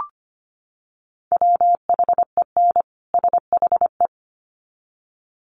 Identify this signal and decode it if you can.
Morse code, heard as a keyed tone, about 25 wpm, 700 Hz. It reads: WHEN SHE
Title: Sets of 2 Words 25wpm